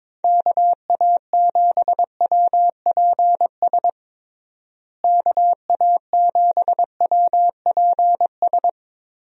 Morse code, heard as a keyed tone, about 22 wpm, 700 Hz. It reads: KA7WPS KA7WPS
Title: USA Call Signs - 6 Characters - Repeated 2x - 22wpm